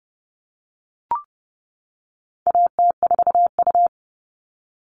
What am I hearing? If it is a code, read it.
Morse code, heard as a keyed tone, about 30 wpm, 700 Hz. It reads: AT4U
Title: All Cut Numbers - 4-Digit 30wpm